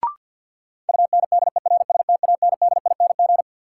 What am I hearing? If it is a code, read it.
Morse code, heard as a keyed tone, about 53 wpm, 700 Hz. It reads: UNDERSTANDING
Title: Sentences from Top 1000 Words - Review Part 3 - New Word Drill - 50wpm